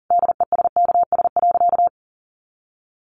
Morse code, heard as a keyed tone, about 37 wpm, 700 Hz. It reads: DESKS.
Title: Additional Easy Phrases 40wpm